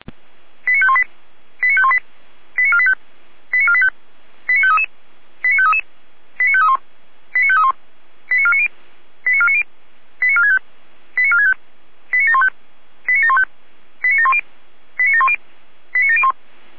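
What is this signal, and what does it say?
Five-tone selective call (ZVEI), heard as quick runs of five tones, tones of about 70 ms. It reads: 89518 89518 89485 89485 89533 89533 89531 89531 89490 89490 89465 89465 89715 89715 89710 89710 89701
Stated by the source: ZVEI five-tone selective call on 455, received by SDR.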